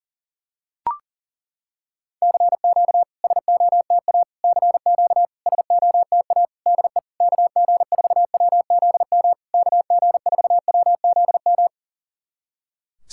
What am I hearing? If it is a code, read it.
Morse code, heard as a keyed tone, about 40 wpm, 700 Hz. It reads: CQ SOTA CQ SOTA DE KG4WZM KG4WZM
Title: SOTA QSO Practice 40wpm